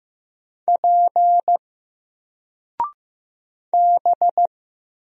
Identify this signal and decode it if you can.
Morse code, heard as a keyed tone, about 15 wpm, 700 Hz. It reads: P B
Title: Single Letter-Number 15wpm